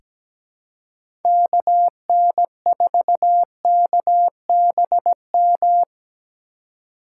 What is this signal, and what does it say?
Morse code, heard as a keyed tone, about 17 wpm, 700 Hz. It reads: KN4KBM
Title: USA Call Signs - 2x Character Spacing - 17wpm